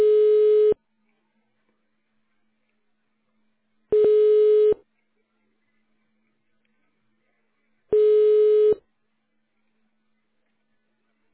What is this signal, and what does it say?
Ringback tone (425 Hz, 1 s on, 3 s off).
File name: 1040.3000.732ff9a2-5e95-436d-bce0